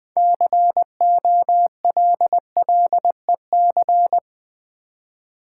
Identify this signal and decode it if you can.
Morse code, heard as a keyed tone, about 20 wpm, 700 Hz. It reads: COLLEC
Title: Build A Word - 3 Letters At A Time - 9-Letter Words - 20wpm